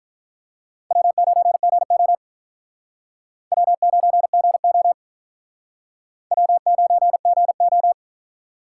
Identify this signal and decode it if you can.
Morse code, heard as a keyed tone, about 49 wpm, 700 Hz. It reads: W9GO W9GO W9GO